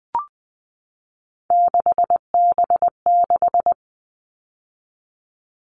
Morse code, heard as a keyed tone, about 20 wpm, 700 Hz. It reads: 6B6
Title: All Cut Numbers - 3-Digit 20wpm